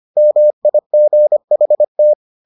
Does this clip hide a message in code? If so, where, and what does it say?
Yes: Morse code from 0.2 to 2.1 s, heard as a keyed tone, about 25 wpm, 600 Hz. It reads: MIGHT